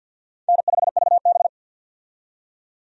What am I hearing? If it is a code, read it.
Morse code, heard as a keyed tone, about 53 wpm, 700 Hz. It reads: N5VB
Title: USA Call Signs - 4 Characters - 50wpm